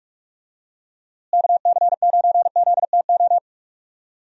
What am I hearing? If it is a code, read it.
Morse code, heard as a keyed tone, about 46 wpm, 700 Hz. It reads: KC9ZTO
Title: USA Call Signs - 6 Characters - Repeated 3x - 45wpm